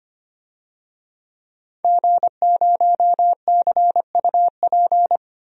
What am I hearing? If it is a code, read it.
Morse code, heard as a keyed tone, about 25 wpm, 700 Hz. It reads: G0CUP
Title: DXCC-100 Call Signs - 5 Characters - Repeated 3x - 25wpm